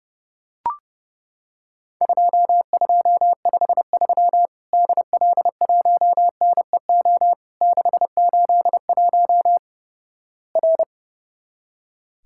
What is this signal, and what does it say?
Morse code, heard as a keyed tone, about 30 wpm, 700 Hz. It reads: 2253 DL1NEO 681 R
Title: Worked All Europe DX Contest - QTC Exchange - 30wpm